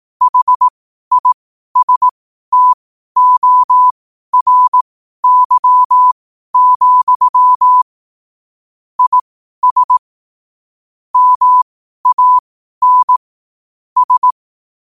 Morse code, heard as a keyed tone, about 18 wpm, spaced as for 12 wpm, 1000 Hz. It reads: HISTORY, IS MAN S
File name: qotdfiles / 12wpm